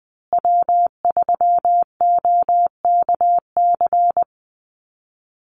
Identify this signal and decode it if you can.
Morse code, heard as a keyed tone, about 19 wpm, 700 Hz. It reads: W3OKC
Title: Call Signs Repeated 3x - 20wpm